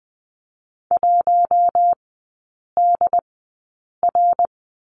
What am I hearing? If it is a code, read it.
Morse code, heard as a keyed tone, about 19 wpm, spaced as for 8 wpm, 700 Hz. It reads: 1DR
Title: 166 Lesson 27e - Number 7 - Three Characters 20wpm